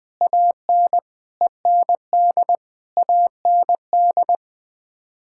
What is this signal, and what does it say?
Morse code, heard as a keyed tone, about 20 wpm, 700 Hz. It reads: AN END AND
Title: Sets of 9 Words - 20wpm